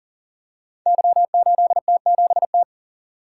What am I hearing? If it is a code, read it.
Morse code, heard as a keyed tone, about 40 wpm, 690 Hz. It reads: Y8T7T
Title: Code Groups - 5 Letter:Numbers - 40wpm